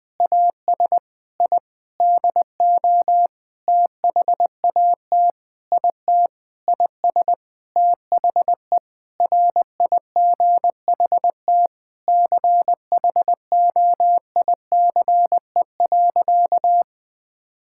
Morse code, heard as a keyed tone, about 19 wpm, 700 Hz. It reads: AS I DO THAT IT IS THE RIGHT CHOICE.
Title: Sentences with 8-Word Phrases - 20wpm